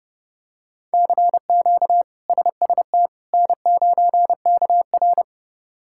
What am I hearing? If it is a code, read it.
Morse code, heard as a keyed tone, about 30 wpm, 700 Hz. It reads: CQ SST N9KR
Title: K1USN SST QSO Practice - 30wpm